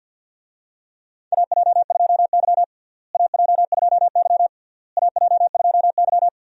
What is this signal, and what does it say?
Morse code, heard as a keyed tone, about 53 wpm, 700 Hz. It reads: AJ2Y AJ2Y AJ2Y
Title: USA Call Signs - 4 Characters - Repeated 3x - 50wpm